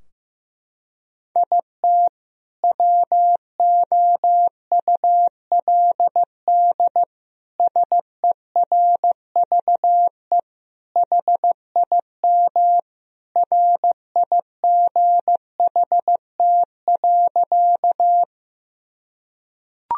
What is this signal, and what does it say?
Morse code, heard as a keyed tone, about 15 wpm, 700 Hz. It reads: IT WOULD SERVE HIM RIGHT.
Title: Sentences from Top 1000 Words - Review Part 2 - 15wpm